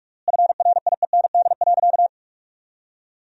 Morse code, heard as a keyed tone, about 46 wpm, 700 Hz. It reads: FRIEND.
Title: 3x Word-Spacing - Sentences from Top 300 Words - 45wpm